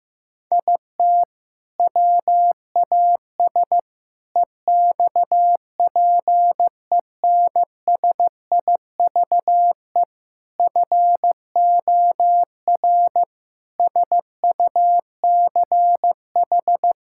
Morse code, heard as a keyed tone, about 16 wpm, 700 Hz. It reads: IT WAS EXPENSIVE FOR SUCH
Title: Sentences from Top 1000 Words - Review Part 6 - 15wpm